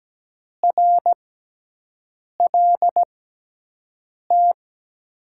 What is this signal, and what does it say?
Morse code, heard as a keyed tone, about 17 wpm, spaced as for 6 wpm, 700 Hz. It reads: RLT
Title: USA Call Signs - 6 Characters - 6x Character Spacing - 17wpm